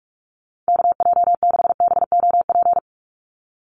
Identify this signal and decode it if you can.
Morse code, heard as a keyed tone, about 44 wpm, 700 Hz. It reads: KJ6BOP
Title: USA Call Signs 45wpm